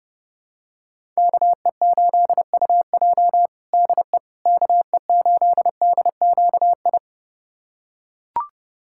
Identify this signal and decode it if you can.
Morse code, heard as a keyed tone, about 32 wpm, 700 Hz. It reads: KE8UJ DE KE8DQI
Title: USA Callsign DE Callsign 30wpm